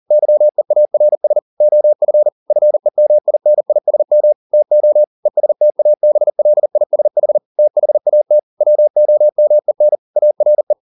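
Morse code, heard as a keyed tone, about 40 wpm, 600 Hz. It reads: YEARS OF FEMINISM TO ESTABLISH THAT WOMEN ARE
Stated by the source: Morse code at 40 wpm, practice text, CW med Gnister.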